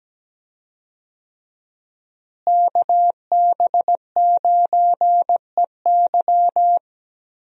Morse code, heard as a keyed tone, about 17 wpm, 700 Hz. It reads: KB9EY
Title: USA Call Signs - 5 Characters - Repeated 2x - 17wpm